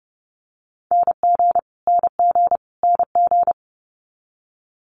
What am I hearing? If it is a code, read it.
Morse code, heard as a keyed tone, about 30 wpm, 700 Hz. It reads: NG NG NG
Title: Multi-letter Phonemes Repeated 3x 30wpm